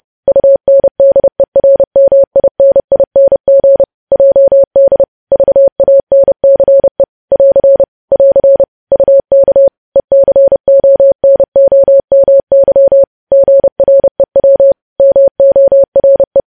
Morse code, heard as a keyed tone, about 30 wpm, 550 Hz. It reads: UNDERMINING JD VANCE <AR> <AR> UK ECONOMY GREW MORE
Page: news headlines in morse code at 30 wpm